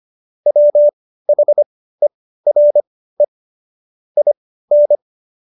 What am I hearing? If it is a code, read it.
Morse code, heard as a keyed tone, about 25 wpm, spaced as for 15 wpm, 590 Hz. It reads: WHERE IN